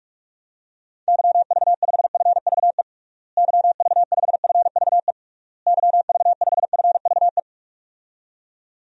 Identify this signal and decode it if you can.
Morse code, heard as a keyed tone, about 46 wpm, 700 Hz. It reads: YV5FVE YV5FVE YV5FVE
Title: DXCC-100 Call Signs - 6 Characters - Repeated 3x - 45wpm